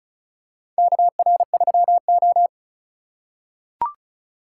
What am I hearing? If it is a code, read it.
Morse code, heard as a keyed tone, about 36 wpm, 700 Hz. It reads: KR3O
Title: USA Call Signs - 4 Characters - 8x Character Spacing - 35wpm